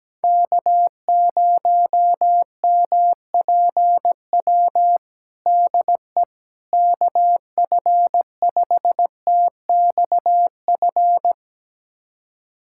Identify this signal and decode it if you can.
Morse code, heard as a keyed tone, about 17 wpm, 700 Hz. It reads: K0MPW DE KF5TXF
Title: USA Callsign DE Callsign 17wpm